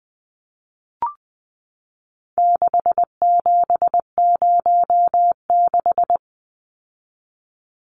Morse code, heard as a keyed tone, about 19 wpm, 700 Hz. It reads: 6706
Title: Numbers - 4-Digit 20wpm